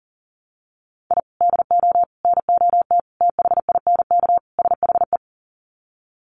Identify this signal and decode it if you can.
Morse code, heard as a keyed tone, about 40 wpm, 700 Hz. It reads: I DO NOT THINK SHE
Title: Sets of 5 Words 40wpm